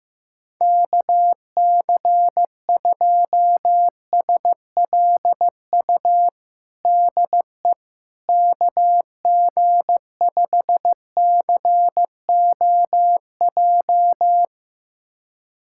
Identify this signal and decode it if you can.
Morse code, heard as a keyed tone, about 15 wpm, 700 Hz. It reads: KC2SLU DE KG5COJ
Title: USA Callsign DE Callsign - 6 Characters - 15wpm